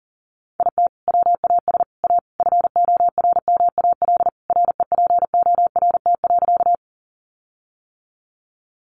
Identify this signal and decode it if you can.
Morse code, heard as a keyed tone, about 40 wpm, 700 Hz. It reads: IT WAS A FORMAL REPORT.
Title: Sentences from Top 1300 Words 40wpm